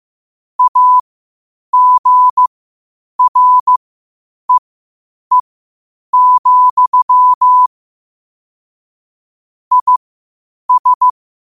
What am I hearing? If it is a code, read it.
Morse code, heard as a keyed tone, about 15 wpm, spaced as for 8 wpm, 1000 Hz. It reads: AGREE, IS
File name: qotdfiles / 8wpm